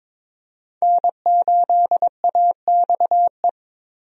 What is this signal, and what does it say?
Morse code, heard as a keyed tone, about 23 wpm, 700 Hz. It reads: N8AXE
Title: USA Call Signs - 5 Characters - Repeated 3x - 22wpm